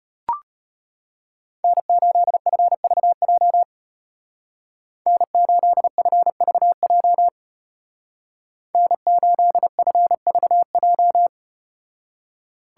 Morse code, heard as a keyed tone, about 33 wpm, 700 Hz. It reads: N8FVJ N8FVJ N8FVJ